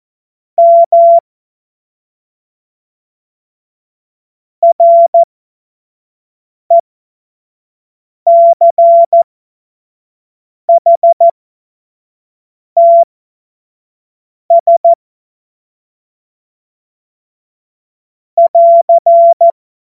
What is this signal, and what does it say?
Morse code, heard as a keyed tone, about 14 wpm, spaced as for 5 wpm, 680 Hz. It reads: M RECHTS <AR>
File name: Morsekurs / Witze / Witz 06. April 2021, 25 F70